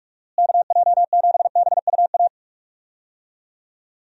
Morse code, heard as a keyed tone, about 46 wpm, 700 Hz. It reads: KJ7BUA
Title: USA Call Signs - 6 Characters - Repeated 3x - 45wpm